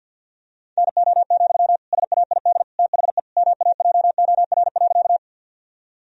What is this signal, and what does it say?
Morse code, heard as a keyed tone, about 50 wpm, 700 Hz. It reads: NO, SAID THE MAJOR.
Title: Sentences from Top 1000 Words - Review Part 4 - 50wpm